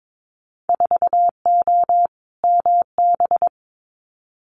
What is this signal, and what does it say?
Morse code, heard as a keyed tone, about 22 wpm, 700 Hz. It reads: 4O MB
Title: Winter Field Day Exchanges Encore 22wpm